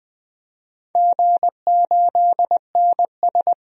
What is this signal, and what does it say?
Morse code, heard as a keyed tone, about 20 wpm, 700 Hz. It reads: G8NS
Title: DXCC-100 Call Signs - 4 Characters - 20wpm